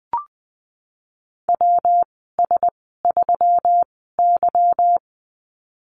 Morse code, heard as a keyed tone, about 20 wpm, spaced as for 15 wpm, 700 Hz. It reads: WS3Y
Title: 161 Lesson 26g - Letter g - Call Signs 20wpm